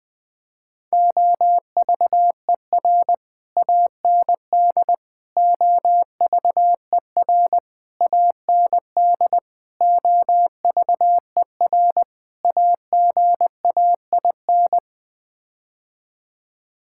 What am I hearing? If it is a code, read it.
Morse code, heard as a keyed tone, about 21 wpm, 700 Hz. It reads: OVER AND OVER AND OVER AGAIN
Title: Sets of 6 Words - 20wpm